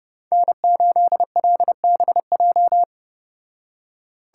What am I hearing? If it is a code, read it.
Morse code, heard as a keyed tone, about 33 wpm, 700 Hz. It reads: N8LBJ